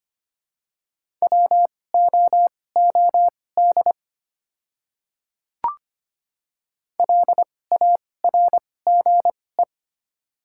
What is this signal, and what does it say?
Morse code, heard as a keyed tone, about 25 wpm, spaced as for 18 wpm, 700 Hz. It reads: WOOD LARGE
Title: 2x Character Spacing - Top 300 Words - 25wpm